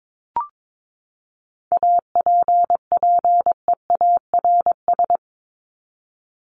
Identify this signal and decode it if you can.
Morse code, heard as a keyed tone, about 22 wpm, 700 Hz. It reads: APPEARS
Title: ES:S Suffix - Top 100 Words with Base Words 22wpm